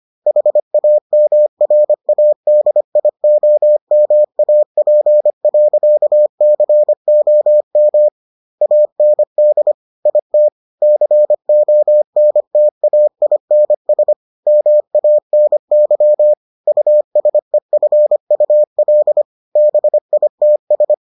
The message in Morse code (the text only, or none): HAMRADIOMAP.COM AND IT CONTAINS MANY USEFUL BITS